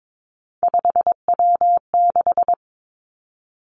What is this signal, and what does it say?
Morse code, heard as a keyed tone, about 22 wpm, 700 Hz. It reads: 5W6
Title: Three Challenging Letter-Number Combinations 22wpm